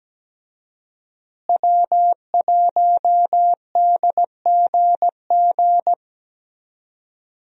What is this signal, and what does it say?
Morse code, heard as a keyed tone, about 17 wpm, 700 Hz. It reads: W1DGG
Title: USA Call Signs - 5 Characters - Repeated 3x - 17wpm